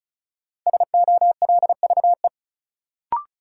Morse code, heard as a keyed tone, about 36 wpm, 700 Hz. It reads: SOLVE
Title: Sentences from Top 1000 Words - New Word Drill - 35wpm